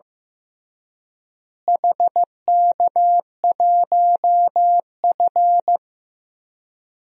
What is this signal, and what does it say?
Morse code, heard as a keyed tone, about 15 wpm, 700 Hz. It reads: HK1F
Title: DXCC-100 Call Signs - 4 Characters - Repeated 3x - 15wpm